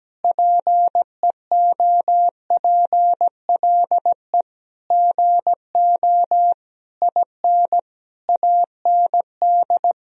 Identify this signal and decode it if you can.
Morse code, heard as a keyed tone, about 17 wpm, 700 Hz. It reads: PEOPLE GO IN AND
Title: Sentences from Top 1000 Words - Review Part 2 - 17wpm